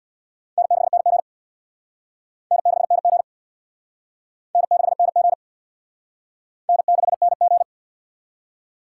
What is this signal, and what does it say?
Morse code, heard as a keyed tone, about 66 wpm, 690 Hz. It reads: N6NG N6NG N6NG N6NG